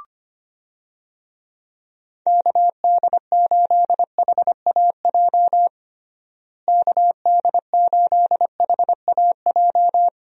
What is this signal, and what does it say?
Morse code, heard as a keyed tone, about 25 wpm, 700 Hz. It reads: KD8HAJ KD8HAJ
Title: USA Call Signs - 6 Characters - Repeated 3x - 25wpm